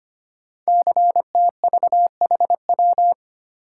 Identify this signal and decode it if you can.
Morse code, heard as a keyed tone, about 25 wpm, 700 Hz. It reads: CTVHW
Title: Code Groups - 5 Letters - 25wpm